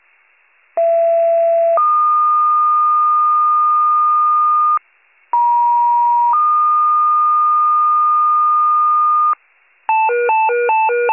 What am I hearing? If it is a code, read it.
Two-tone page, heard as a one-second tone, then a three-second tone, about 670 then 1220 Hz, about 950 then 1220 Hz. It is heard from 0.8 to 9.3 s.